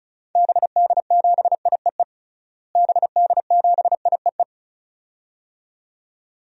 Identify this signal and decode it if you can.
Morse code, heard as a keyed tone, about 36 wpm, 700 Hz. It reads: BD7IEE BD7IEE
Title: DXCC-100 Call Signs - 6 Characters - Repeated 2x - 35wpm